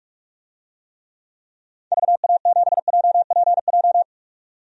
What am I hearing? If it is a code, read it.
Morse code, heard as a keyed tone, about 46 wpm, 700 Hz. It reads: VA7JPJ
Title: DXCC-100 Call Signs - 6 Characters - 45wpm